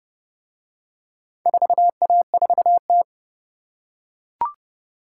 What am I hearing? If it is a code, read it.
Morse code, heard as a keyed tone, about 28 wpm, 700 Hz. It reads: 4A4T